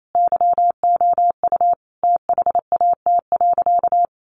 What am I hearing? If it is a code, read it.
Morse code, heard as a keyed tone, about 28 wpm, 700 Hz. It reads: YOU THAT.
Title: Sentences from Top 300 Words 28wpm